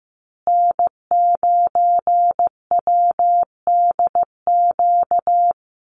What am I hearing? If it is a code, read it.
Morse code, heard as a keyed tone, about 15 wpm, 700 Hz. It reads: N9WDQ
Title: USA Call Signs Repeated 2x 15wpm